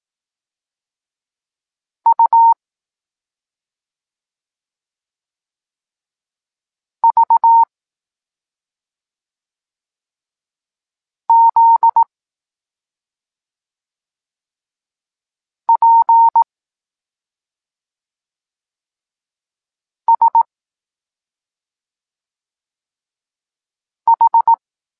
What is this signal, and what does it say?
Morse code, heard as a keyed tone, about 18 wpm, 920 Hz. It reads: U V Z P S H